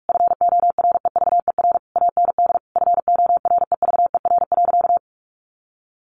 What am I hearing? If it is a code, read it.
Morse code, heard as a keyed tone, about 44 wpm, 700 Hz. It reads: FOREVER AND FOREVER.
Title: A Sound Of Thunder 45wpm